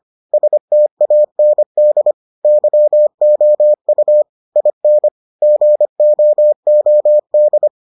Morse code, heard as a keyed tone, about 25 wpm, 600 Hz. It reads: STAND YOU IN GOOD